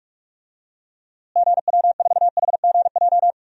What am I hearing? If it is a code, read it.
Morse code, heard as a keyed tone, about 46 wpm, 700 Hz. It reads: GW4HGJ
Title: DXCC-100 Call Signs - 6 Characters - Repeated 2x - 45wpm